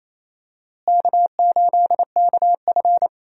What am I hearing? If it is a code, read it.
Morse code, heard as a keyed tone, about 27 wpm, 700 Hz. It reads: K8KF
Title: USA Call Signs - 4 Characters - Repeated 2x - 28wpm